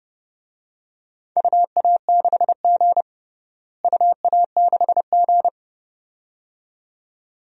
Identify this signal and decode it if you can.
Morse code, heard as a keyed tone, about 30 wpm, 700 Hz. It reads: UA6G UA6G
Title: DXCC-100 Call Signs - 4 Characters - Repeated 2x - 30wpm